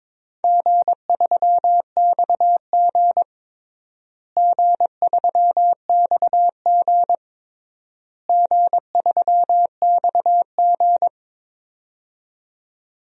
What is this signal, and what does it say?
Morse code, heard as a keyed tone, about 22 wpm, 700 Hz. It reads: G3XG G3XG G3XG
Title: DXCC-100 Call Signs - 4 Characters - Repeated 3x - 22wpm